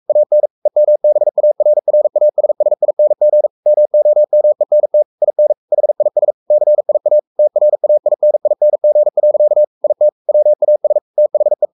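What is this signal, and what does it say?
Morse code, heard as a keyed tone, about 44 wpm, 600 Hz. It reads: AN EMBARRASSING MOMENT IN HIS CIA TRAINING. IT WAS THE
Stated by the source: Morse code at 43 wpm sending Ken Follett's novel Never in English, from CW med Gnister.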